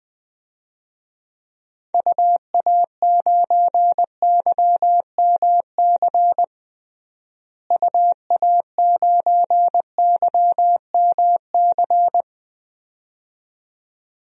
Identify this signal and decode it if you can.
Morse code, heard as a keyed tone, about 20 wpm, 700 Hz. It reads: UA9YMC UA9YMC
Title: DXCC-100 Call Signs - 6 Characters - Repeated 2x - 20wpm